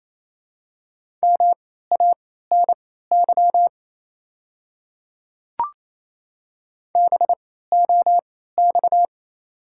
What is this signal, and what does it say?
Morse code, heard as a keyed tone, about 28 wpm, spaced as for 16 wpm, 700 Hz. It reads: MANY BOX
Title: 3x Character Spacing - Top 500 Words - 28wpm